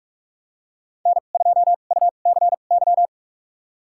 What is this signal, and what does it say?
Morse code, heard as a keyed tone, about 46 wpm, spaced as for 33 wpm, 700 Hz. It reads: N2UCY
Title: USA Call Signs - 2x Character Spacing - 45wpm